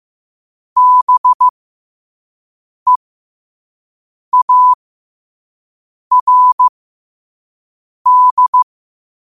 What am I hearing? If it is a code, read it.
Morse code, heard as a keyed tone, about 15 wpm, spaced as for 5 wpm, 1000 Hz. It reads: BEARD